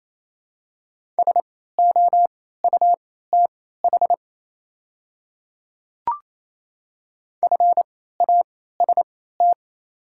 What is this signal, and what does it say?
Morse code, heard as a keyed tone, about 28 wpm, spaced as for 16 wpm, 700 Hz. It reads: SOUTH FAST
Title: 3x Character Spacing - Top 500 Words - 28wpm